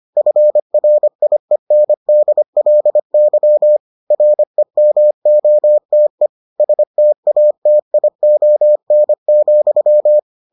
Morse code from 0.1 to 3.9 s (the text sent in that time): FRIENDLY